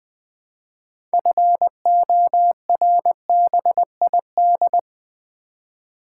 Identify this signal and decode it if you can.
Morse code, heard as a keyed tone, about 20 wpm, 700 Hz. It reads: FORBID
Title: Build A Word - 3 Letters At A Time - 6-Letter Words - 20wpm